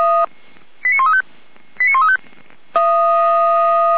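Five-tone selective call (ZVEI), heard as quick runs of five tones, tones of about 70 ms. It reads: 89135 89135
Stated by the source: ZVEI five-tone selective call received by SDR, logged on the audio server of Feuerwehr Orlamünde fire brigade.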